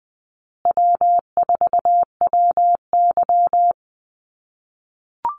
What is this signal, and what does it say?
Morse code, heard as a keyed tone, about 20 wpm, 700 Hz. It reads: W4WY
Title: USA Call Signs Repeated 3x 20wpm